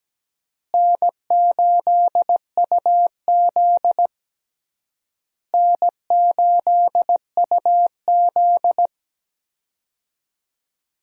Morse code, heard as a keyed tone, about 17 wpm, 700 Hz. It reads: N8UZ N8UZ
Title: USA Call Signs - 4 Characters - Repeated 3x - 17wpm